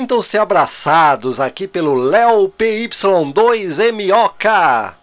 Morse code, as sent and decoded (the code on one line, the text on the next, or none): none